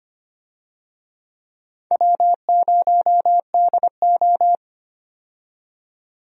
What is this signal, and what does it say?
Morse code, heard as a keyed tone, about 25 wpm, 700 Hz. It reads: W0DO
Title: USA Call Signs - 4 Characters - 25wpm